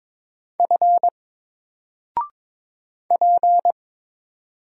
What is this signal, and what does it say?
Morse code, heard as a keyed tone, about 22 wpm, 700 Hz. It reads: F P
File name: Single-Letter-Number-22wpm